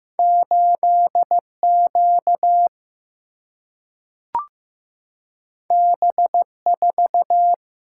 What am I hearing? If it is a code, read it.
Morse code, heard as a keyed tone, about 15 wpm, 700 Hz. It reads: 8Q B4
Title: Single-Letter Single-Number 15wpm